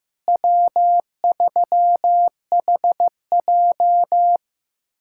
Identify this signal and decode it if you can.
Morse code, heard as a keyed tone, about 15 wpm, 700 Hz. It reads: W3HJ
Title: USA Callsign DE Callsign - 4 Characters - 15wpm